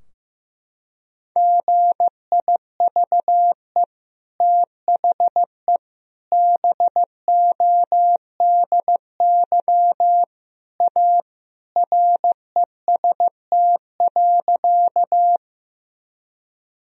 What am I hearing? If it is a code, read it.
Morse code, heard as a keyed tone, about 16 wpm, 700 Hz. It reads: GIVE THE BODY A REST.
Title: Sentences from Top 1000 Words - Review Part 2 - 15wpm